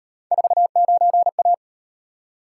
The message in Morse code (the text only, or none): 49A